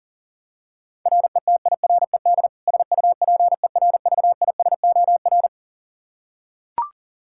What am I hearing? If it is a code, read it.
Morse code, heard as a keyed tone, about 40 wpm, 700 Hz. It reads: RETIRED SUPERVISOR
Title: K1USN SST QSO Practice 40wpm